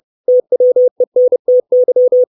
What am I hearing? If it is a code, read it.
Morse code, heard as a keyed tone, about 30 wpm, 500 Hz. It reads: TWENTY
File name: 632 Words 30WPM